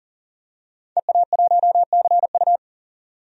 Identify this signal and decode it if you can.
Morse code, heard as a keyed tone, about 40 wpm, 700 Hz. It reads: EA1CU
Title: DXCC-100 Call Signs - 5 Characters - 40wpm